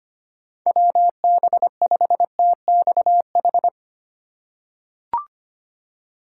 Morse code, heard as a keyed tone, about 25 wpm, 700 Hz. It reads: WB5TXH
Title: USA Call Signs - 6 Characters - 4x Character Spacing - 25wpm